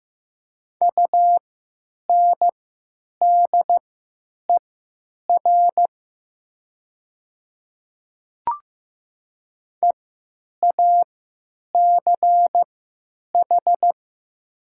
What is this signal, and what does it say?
Morse code, heard as a keyed tone, about 15 wpm, spaced as for 9 wpm, 700 Hz. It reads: UNDER EACH
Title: 3x Character Spacing - Top 200 Words - 15wpm